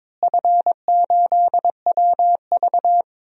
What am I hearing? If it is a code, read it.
Morse code, heard as a keyed tone, about 22 wpm, 700 Hz. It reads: F8WV
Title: DXCC-100 Call Signs - 4 Characters - Repeated 3x - 22wpm